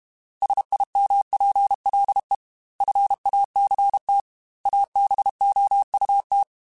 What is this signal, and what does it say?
Morse code, heard as a keyed tone, about 32 wpm, 800 Hz. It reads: SIMPLE FACT ABOUT
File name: Technical Subjects / Technical Subjects-mp3s / OR Gate @30